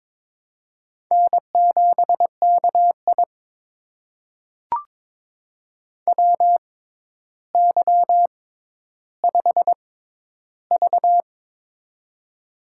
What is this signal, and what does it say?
Morse code, heard as a keyed tone, about 22 wpm, spaced as for 11 wpm, 700 Hz. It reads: N7KI WY5V
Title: USA Call Signs - 4 Characters - 6x Character Spacing - 22wpm